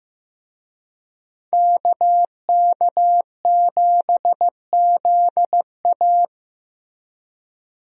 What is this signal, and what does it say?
Morse code, heard as a keyed tone, about 15 wpm, 700 Hz. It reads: KK7ZA
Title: USA Call Signs - 5 Characters - Repeated 3x - 15wpm